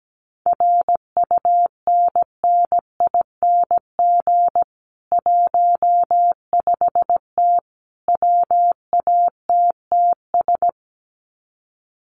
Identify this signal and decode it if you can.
Morse code, heard as a keyed tone, about 17 wpm, 700 Hz. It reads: RUNNING 15T WATTS
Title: Common QSO Phrases 17wpm